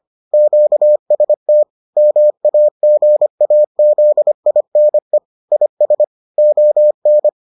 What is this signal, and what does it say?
Morse code, heard as a keyed tone, about 25 wpm, 600 Hz. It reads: QST MAGAZINE IS ON